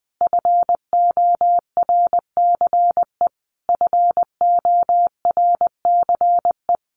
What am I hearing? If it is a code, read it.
Morse code, heard as a keyed tone, about 20 wpm, 700 Hz. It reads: FORCE FORCE
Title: Top 500 Words Repeated 20wpm